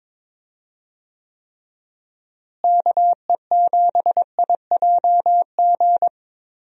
Morse code, heard as a keyed tone, about 22 wpm, 700 Hz. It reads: KE7IJG
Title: USA Call Signs - 6 Characters - Repeated 2x - 22wpm